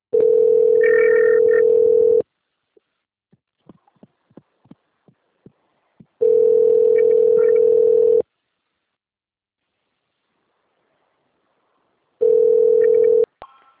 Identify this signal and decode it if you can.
Ringback tone (North America), 440 + 480 Hz.